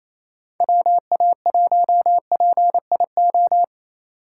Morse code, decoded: WA1PIO

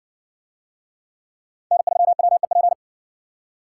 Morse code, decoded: N3WEP

